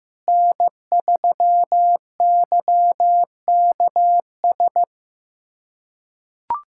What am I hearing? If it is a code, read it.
Morse code, heard as a keyed tone, about 15 wpm, 700 Hz. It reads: N3YKS
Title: USA Callsign DE Callsign - 5 Characters - 15wpm